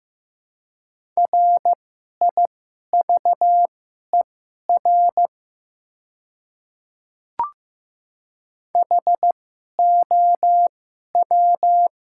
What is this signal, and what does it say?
Morse code, heard as a keyed tone, about 15 wpm, spaced as for 11 wpm, 700 Hz. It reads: RIVER HOW
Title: 2x Character Spacing - Top 300 Words - 15wpm